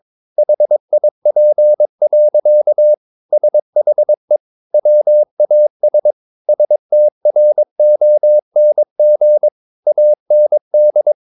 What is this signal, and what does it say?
Morse code, heard as a keyed tone, about 22 wpm, 600 Hz. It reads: HIP. SHE WAS STRONG AND
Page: Romanen Never af Ken Follet på engelsk.